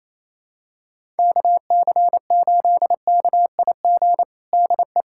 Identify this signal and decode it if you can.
Morse code, heard as a keyed tone, about 28 wpm, 700 Hz. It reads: KC8KIG DE
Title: USA Callsign DE Callsign - 6 Characters - 28wpm